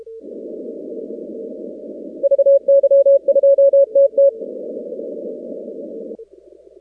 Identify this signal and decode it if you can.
Morse code, heard as a keyed tone, about 32 wpm, 560 Hz. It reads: VY2TT
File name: contests / 2019-cw-wpx-kt5j-k5pi- -2330 0-19